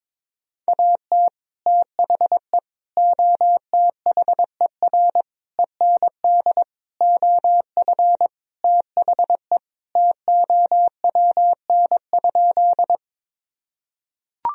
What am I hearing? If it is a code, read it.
Morse code, heard as a keyed tone, about 21 wpm, 700 Hz. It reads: AT THE OTHER END OF THE TOWN?
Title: Sentences from Top 1000 Words - Review Part 2 - 22wpm